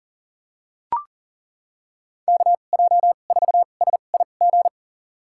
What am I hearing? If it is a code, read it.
Morse code, heard as a keyed tone, about 40 wpm, spaced as for 29 wpm, 700 Hz. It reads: KJ4SIG
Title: USA Call Signs - 6 Characters - 2x Character Spacing - 40wpm